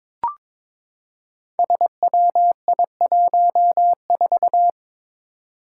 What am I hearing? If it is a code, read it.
Morse code, heard as a keyed tone, about 23 wpm, 700 Hz. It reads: SWI14
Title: Five Challenging Letter-Number Combinations 22wpm